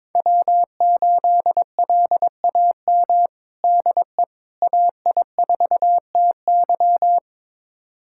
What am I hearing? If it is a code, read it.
Morse code, heard as a keyed tone, about 22 wpm, 700 Hz. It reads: W8LAM DE AI4TY
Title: USA Callsign DE Callsign - 5 Characters - 22wpm